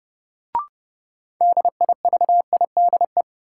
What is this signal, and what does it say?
Morse code, heard as a keyed tone, about 31 wpm, 700 Hz. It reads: DIVIDE
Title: Sentences from Top 1600 Words - New Word Drill Alt - 30wpm